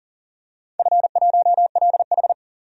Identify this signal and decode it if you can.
Morse code, heard as a keyed tone, about 40 wpm, 700 Hz. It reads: F1LH
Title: DXCC-100 Call Signs - 4 Characters - 40wpm